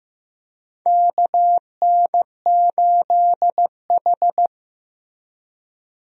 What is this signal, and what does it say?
Morse code, heard as a keyed tone, about 15 wpm, 700 Hz. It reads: KN8H
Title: USA Call Signs - 4 Characters - Repeated 2x - 15wpm